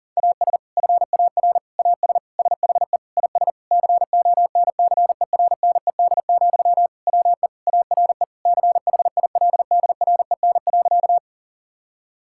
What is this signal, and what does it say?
Morse code, heard as a keyed tone, about 37 wpm, 700 Hz. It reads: AS FAR AS SHE IS CONCERNED, WE ARE CHILDREN.
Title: Sentences from Top 1000 Words - Review Part 6 - 40wpm